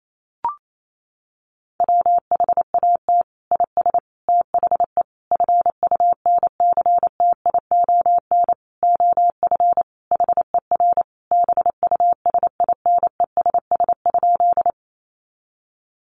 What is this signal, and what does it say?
Morse code, heard as a keyed tone, about 28 wpm, 700 Hz. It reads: WHAT IS THE FUNCTION OF HER BUSINESS?
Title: Sentences from Top 1100 Words 28wpm